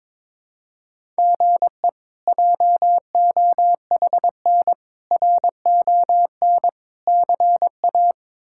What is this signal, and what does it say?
Morse code, heard as a keyed tone, about 22 wpm, 700 Hz. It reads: GE JOHN RON CA
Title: K1USN SST QSO Practice - 22wpm